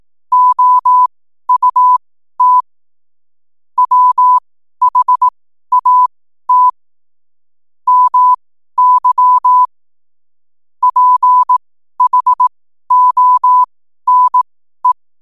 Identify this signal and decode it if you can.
Morse code, heard as a keyed tone, about 18 wpm, spaced as for 12 wpm, 1000 Hz. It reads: OUT WHAT MY PHONE
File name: n7net / 12wpm